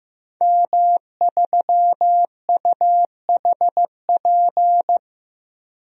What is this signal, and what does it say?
Morse code, heard as a keyed tone, about 15 wpm, 700 Hz. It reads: M3UHP
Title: DXCC-100 Call Signs - 5 Characters - Repeated 2x - 15wpm